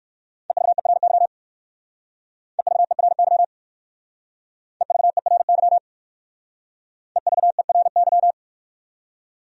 Morse code, heard as a keyed tone, about 55 wpm, 700 Hz. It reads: EVERY EVERY EVERY EVERY
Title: Speed-Racing Top 200 Words 45wpm